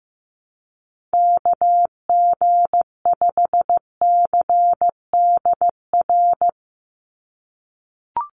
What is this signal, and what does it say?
Morse code, heard as a keyed tone, about 15 wpm, 700 Hz. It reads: KG5CDR
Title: USA Call Signs 15wpm